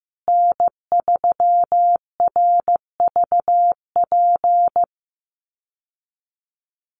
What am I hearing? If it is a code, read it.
Morse code, heard as a keyed tone, about 15 wpm, 700 Hz. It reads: N3RVP
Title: USA Call Signs Repeated 2x 15wpm